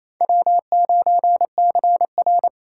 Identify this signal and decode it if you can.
Morse code, heard as a keyed tone, about 34 wpm, 700 Hz. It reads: W9CR
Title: Speed Racing - USA 1x2 Call Signs 28wpm